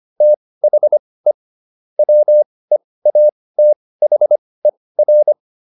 Morse code, heard as a keyed tone, about 25 wpm, spaced as for 18 wpm, 600 Hz. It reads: THE WEATHER